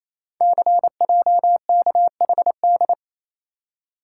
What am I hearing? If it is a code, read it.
Morse code, heard as a keyed tone, about 27 wpm, 700 Hz. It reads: CJKHD
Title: Code Groups - 5 Letter:Numbers - 28wpm